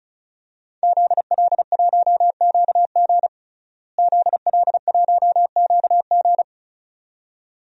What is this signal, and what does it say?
Morse code, heard as a keyed tone, about 34 wpm, 700 Hz. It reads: ZL1QG ZL1QG
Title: DXCC-100 Call Signs - 5 Characters - Repeated 2x - 35wpm